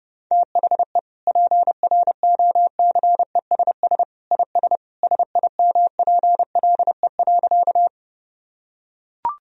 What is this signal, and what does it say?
Morse code, heard as a keyed tone, about 32 wpm, 700 Hz. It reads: THE PROCESS IS SIMPLE.
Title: Sentences from Top 1000 Words - Review Part 3 - 30wpm